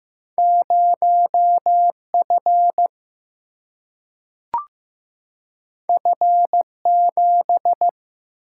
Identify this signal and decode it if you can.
Morse code, heard as a keyed tone, about 15 wpm, 700 Hz. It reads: 0F F7
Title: Single-Letter Single-Number 15wpm